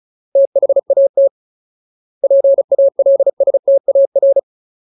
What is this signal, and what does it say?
Morse code, heard as a keyed tone, about 34 wpm, 550 Hz. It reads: THAT PALSTAR
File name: Ultimate Word List / 35WPM Ultimate